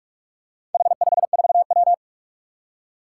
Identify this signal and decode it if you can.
Morse code, heard as a keyed tone, about 46 wpm, 700 Hz. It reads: H54W
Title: Four Challenging Letter-Number Combinations 45wpm